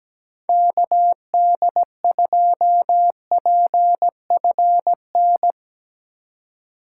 Morse code, heard as a keyed tone, about 17 wpm, 700 Hz. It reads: KD2PFN
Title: USA Call Signs - 6 Characters - 3x Character Spacing - 17wpm